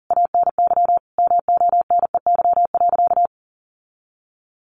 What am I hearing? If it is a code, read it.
Morse code, heard as a keyed tone, about 40 wpm, 700 Hz. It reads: ANY MONEY.
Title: Sentences from Top 1500 Words 40wpm